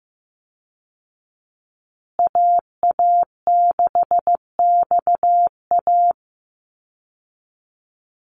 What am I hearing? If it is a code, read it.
Morse code, heard as a keyed tone, about 15 wpm, 700 Hz. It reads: AA6XA
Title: Call Signs Repeated 3x - 15wpm